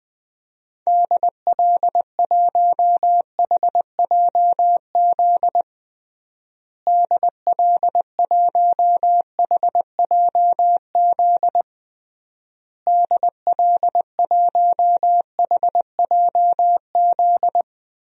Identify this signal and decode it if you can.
Morse code, heard as a keyed tone, about 20 wpm, 700 Hz. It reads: DL1HJZ DL1HJZ DL1HJZ
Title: DXCC-100 Call Signs - 6 Characters - Repeated 3x - 20wpm